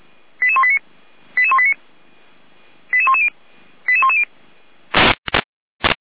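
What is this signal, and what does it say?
Five-tone selective call (ZVEI), heard as quick runs of five tones, tones of about 70 ms. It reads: 88189 88189 88110 88110
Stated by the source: ZVEI five-tone selective call received on an SDR.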